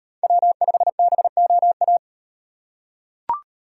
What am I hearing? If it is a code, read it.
Morse code, heard as a keyed tone, about 38 wpm, 700 Hz. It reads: W5BOA